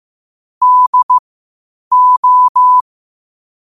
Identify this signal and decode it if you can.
Morse code, heard as a keyed tone, about 15 wpm, spaced as for 8 wpm, 1000 Hz. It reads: DO
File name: qotdfiles / 8wpm